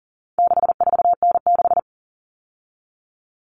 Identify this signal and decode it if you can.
Morse code, heard as a keyed tone, about 40 wpm, 700 Hz. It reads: 64N6
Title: All Cut Numbers - 4-Digit 40wpm